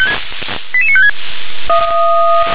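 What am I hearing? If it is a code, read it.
Five-tone selective call (ZVEI), heard as quick runs of five tones, tones of about 70 ms. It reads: 88956
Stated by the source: ZVEI five-tone selective call received on SDR.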